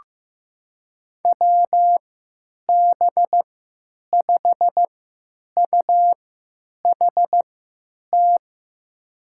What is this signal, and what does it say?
Morse code, heard as a keyed tone, about 15 wpm, spaced as for 9 wpm, 700 Hz. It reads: WB5UHT
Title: USA Call Signs - 3x Character Spacing - 15wpm